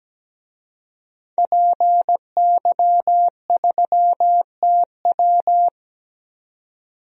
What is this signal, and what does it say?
Morse code, heard as a keyed tone, about 17 wpm, 700 Hz. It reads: PY3TW
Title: DXCC-100 Call Signs - 5 Characters - 17wpm